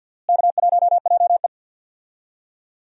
Morse code, heard as a keyed tone, about 52 wpm, 700 Hz. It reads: K1JE
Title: USA Call Signs - 4 Characters - Repeated 3x - 50wpm